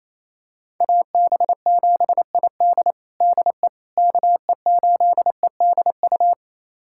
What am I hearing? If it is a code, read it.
Morse code, heard as a keyed tone, about 27 wpm, 700 Hz. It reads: AB7ID DE KE8EDU
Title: USA Callsign DE Callsign 28wpm